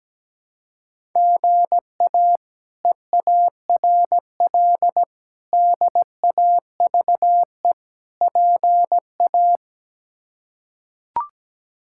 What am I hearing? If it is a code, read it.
Morse code, heard as a keyed tone, about 17 wpm, 700 Hz. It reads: GA EARL DAVE PA
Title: K1USN SST QSO Practice - 17wpm